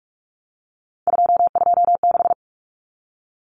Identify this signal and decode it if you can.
Morse code, heard as a keyed tone, about 44 wpm, 700 Hz. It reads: 226
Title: Numbers - 3-Digit 45wpm